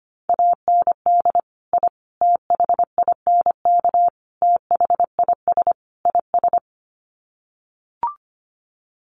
Morse code, heard as a keyed tone, about 25 wpm, 700 Hz. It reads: AND I THINK THIS IS
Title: Sets of 5 Words Encore 25wpm